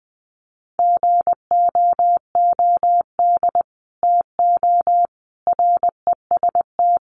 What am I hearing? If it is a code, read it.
Morse code, heard as a keyed tone, about 20 wpm, 700 Hz. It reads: GOOD TO REST
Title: Sentences from Top 500 Words 20wpm